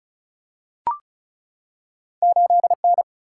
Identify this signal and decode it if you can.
Morse code, heard as a keyed tone, about 36 wpm, 700 Hz. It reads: 8N
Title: Single-Letter Single-Number 35wpm